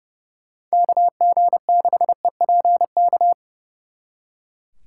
Morse code, heard as a keyed tone, about 38 wpm, 700 Hz. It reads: KG6EPK